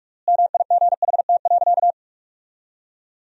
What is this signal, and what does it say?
Morse code, heard as a keyed tone, about 41 wpm, 700 Hz. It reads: MIGHT.
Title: 3x Word-Spacing - Sentences from Top 500 Words - 45wpm